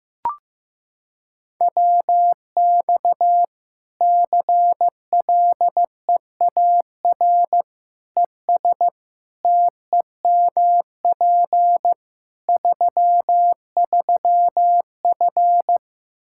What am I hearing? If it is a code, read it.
Morse code, heard as a keyed tone, about 16 wpm, 700 Hz. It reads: WX CLEAR ES TEMP 33F
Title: Practice QSO - Exchange 2 - Info - 15wpm